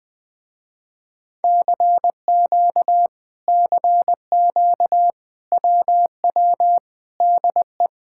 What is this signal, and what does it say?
Morse code, heard as a keyed tone, about 20 wpm, 700 Hz. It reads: CQ CQ WW DE
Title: Speed-Racing Common QSO Phrases - 17wpm